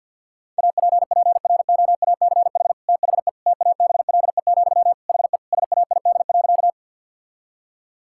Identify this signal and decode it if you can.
Morse code, heard as a keyed tone, about 50 wpm, 700 Hz. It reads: APPROACH THE TABLE, HE SAID.
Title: Sentences from Top 1000 Words - Review Part 6 - 50wpm